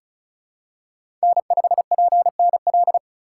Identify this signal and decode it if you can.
Morse code, heard as a keyed tone, about 34 wpm, 700 Hz. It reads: N5PNL
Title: USA Call Signs - 5 Characters - 2x Character Spacing - 35wpm